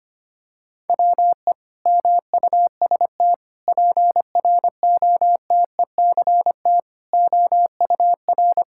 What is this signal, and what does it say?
Morse code, heard as a keyed tone, about 25 wpm, 700 Hz. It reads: WE MUST PROTECT OUR
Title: Sentences from Top 1000 Words - Review Part 5 - 25wpm